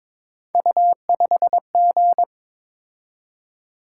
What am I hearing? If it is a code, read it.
Morse code, heard as a keyed tone, about 22 wpm, 700 Hz. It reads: U5G
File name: practice / 22wpm / Three Letter-Number Combinations - 22wpm